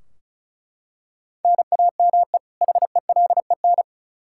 Morse code, heard as a keyed tone, about 36 wpm, 700 Hz. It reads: NAME HELEN
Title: Practice QSO - Exchange 1 - Beginning - 35wpm